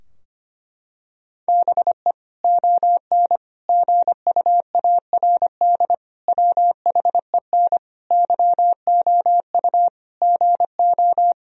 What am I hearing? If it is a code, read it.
Morse code, heard as a keyed tone, about 25 wpm, 700 Hz. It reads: BE ON GUARD WHEN YOU GO
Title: Sentences from Top 1000 Words - Review Part 4 - 25wpm